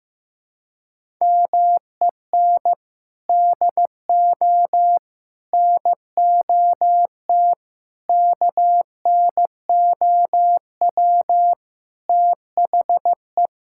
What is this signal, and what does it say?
Morse code, heard as a keyed tone, about 15 wpm, 700 Hz. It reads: MEN DO NOT KNOW THE
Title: Sentences from Top 1000 Words - Review Part 6 - 15wpm